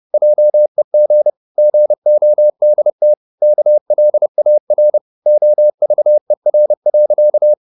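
Morse code, heard as a keyed tone, about 30 wpm, 600 Hz. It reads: JEG GODT KLAR OVER.
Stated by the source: Fast (QRQ) Morse at 29 wpm, from CW med Gnister.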